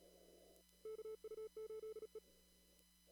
Morse code, heard as a keyed tone, about 37 wpm, 440 Hz. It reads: KU8E
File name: CW / kt5j wpx cw-op-k5pi